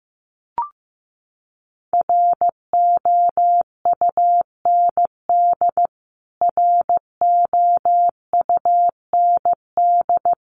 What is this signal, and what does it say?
Morse code, heard as a keyed tone, about 15 wpm, 700 Hz. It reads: ROUND ROUND
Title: Top 500 Words Repeated 15wpm